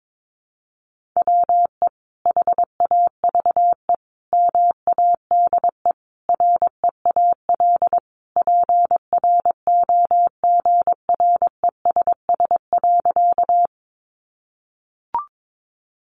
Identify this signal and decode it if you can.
Morse code, heard as a keyed tone, about 22 wpm, 700 Hz. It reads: WE HAVE MADE REAL PROGRESS.
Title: Sentences from Top 1400 Words 22wpm